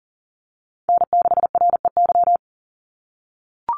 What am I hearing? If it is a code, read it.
Morse code, heard as a keyed tone, about 40 wpm, 700 Hz. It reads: N6REY
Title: USA Call Signs 40wpm